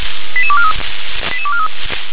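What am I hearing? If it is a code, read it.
Five-tone selective call (ZVEI), heard as quick runs of five tones, tones of about 70 ms. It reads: 88243 88243
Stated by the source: ZVEI five-tone call received via SDR.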